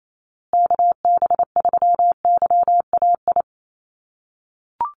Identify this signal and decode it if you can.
Morse code, heard as a keyed tone, about 28 wpm, 700 Hz. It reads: KB3YAI
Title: USA Call Signs Repeated 2x 28wpm